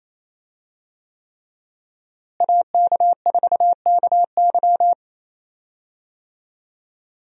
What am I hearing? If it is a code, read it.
Morse code, heard as a keyed tone, about 28 wpm, 700 Hz. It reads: AK4KY
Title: USA Call Signs - 5 Characters - Repeated 3x - 28wpm